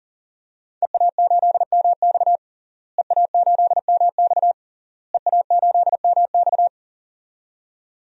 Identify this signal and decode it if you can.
Morse code, heard as a keyed tone, about 40 wpm, 700 Hz. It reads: EA8MX EA8MX EA8MX
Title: DXCC-100 Call Signs - 5 Characters - Repeated 3x - 40wpm